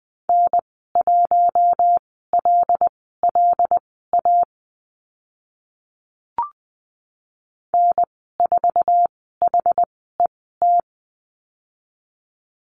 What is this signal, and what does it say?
Morse code, heard as a keyed tone, about 20 wpm, spaced as for 15 wpm, 700 Hz. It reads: N1LLA N4HET
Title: 076 Lesson 13g - Letter L - Call Signs 20wpm